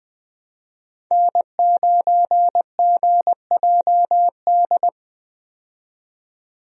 Morse code, heard as a keyed tone, about 20 wpm, 700 Hz. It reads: N9GJD
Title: USA Call Signs - 5 Characters - 20wpm